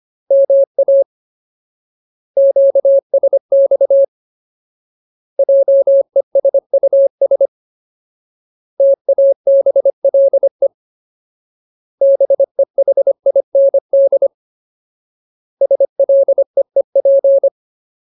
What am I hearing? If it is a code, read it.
Morse code, heard as a keyed tone, about 25 wpm, 550 Hz. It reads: MA QSX JESUS TABLE BEHIND SLEEP
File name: Ultimate Word List / 25WPM Ultimate